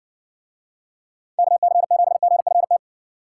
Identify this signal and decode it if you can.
Morse code, heard as a keyed tone, about 55 wpm, 700 Hz. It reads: KC7GFT